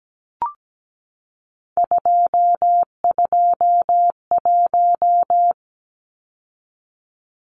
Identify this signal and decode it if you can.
Morse code, heard as a keyed tone, about 17 wpm, 700 Hz. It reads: 221
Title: Numbers - 3-Digit 17wpm